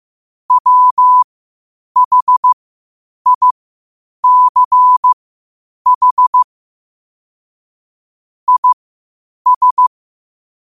Morse code, heard as a keyed tone, about 15 wpm, spaced as for 8 wpm, 1000 Hz. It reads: WHICH IS
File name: qotdfiles / 8wpm